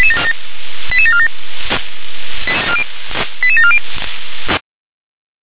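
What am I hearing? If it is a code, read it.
Five-tone selective call (ZVEI), heard as quick runs of five tones, tones of about 70 ms. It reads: 88747 88740
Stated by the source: ZVEI five-tone selective call, code 88645, received by SDR.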